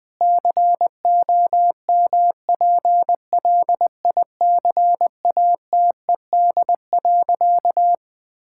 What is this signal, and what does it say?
Morse code, heard as a keyed tone, about 21 wpm, 700 Hz. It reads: COMPLICATED.
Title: Sentences with 6-Word Phrases - 20wpm